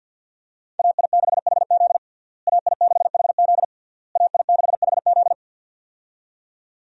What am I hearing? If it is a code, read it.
Morse code, heard as a keyed tone, about 47 wpm, 700 Hz. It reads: AI6HZ AI6HZ AI6HZ
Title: USA Call Signs - 5 Characters - Repeated 3x - 50wpm